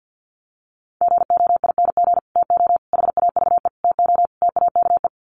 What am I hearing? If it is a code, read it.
Morse code, heard as a keyed tone, about 50 wpm, 700 Hz. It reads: GOING TO HAVE TO TAKE
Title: Sets of 5 Words Encore 50wpm